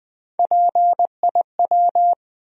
Morse code, heard as a keyed tone, about 20 wpm, 700 Hz. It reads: PIW
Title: USA Call Sign Postfixes - 3 Letters 20wpm